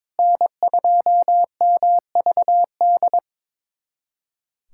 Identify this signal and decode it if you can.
Morse code, heard as a keyed tone, about 21 wpm, 700 Hz. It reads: N2MVD
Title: USA Callsign DE Callsign - 5 Characters - 22wpm